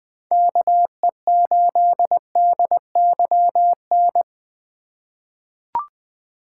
Morse code, heard as a keyed tone, about 20 wpm, 700 Hz. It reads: KE8DYN
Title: USA Call Signs - 6 Characters - 3x Character Spacing - 20wpm